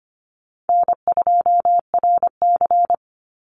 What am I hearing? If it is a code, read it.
Morse code, heard as a keyed tone, about 25 wpm, 700 Hz. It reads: N2RC
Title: Easy Call Signs Repeated 3x - 25wpm